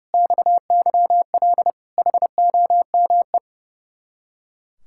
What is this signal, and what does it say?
Morse code, heard as a keyed tone, about 30 wpm, 700 Hz. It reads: XYL HOME
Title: Speed-Racing Common QSO Phrases - 30wpm